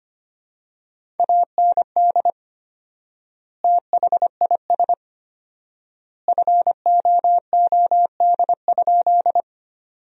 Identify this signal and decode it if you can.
Morse code, heard as a keyed tone, about 25 wpm, 700 Hz. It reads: AND THIS FOOD?
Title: 4x Word-Spacing - Sentences from Top 300 Words - 25wpm